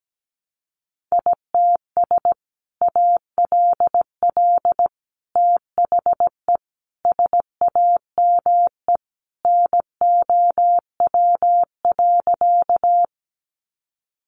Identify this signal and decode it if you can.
Morse code, heard as a keyed tone, about 17 wpm, 700 Hz. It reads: ITS ALL THE SAME NOW.
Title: Sentences from Top 600 Words 17wpm